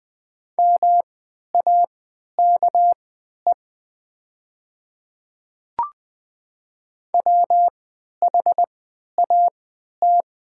Morse code, heard as a keyed tone, about 20 wpm, spaced as for 11 wpm, 700 Hz. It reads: MAKE WHAT
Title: 3x Character Spacing - Top 100 Words - 20wpm